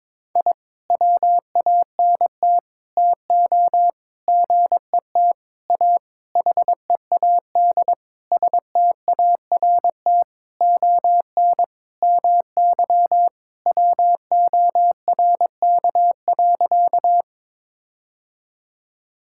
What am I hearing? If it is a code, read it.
Morse code, heard as a keyed tone, about 22 wpm, 700 Hz. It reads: I WANT TO GET A HEAD START ON MY WORK.
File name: Sentences with 7-Word Phrases - 22wpm